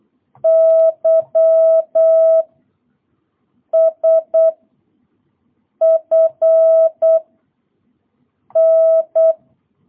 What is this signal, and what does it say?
Morse code, heard as a keyed tone, about 8 wpm, 650 Hz. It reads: Y S F N